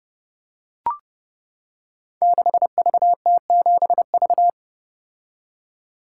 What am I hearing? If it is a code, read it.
Morse code, heard as a keyed tone, about 30 wpm, 700 Hz. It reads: 6VT7V
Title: Code Groups - 5 Letter:Numbers - 30wpm